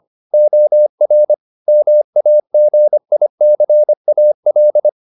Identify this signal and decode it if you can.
Morse code, heard as a keyed tone, about 25 wpm, 600 Hz. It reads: OR MAGICAL